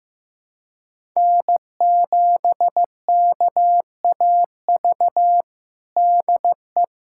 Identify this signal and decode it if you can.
Morse code, heard as a keyed tone, about 16 wpm, 700 Hz. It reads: N7KAV DE
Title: USA Callsign DE Callsign - 5 Characters - 15wpm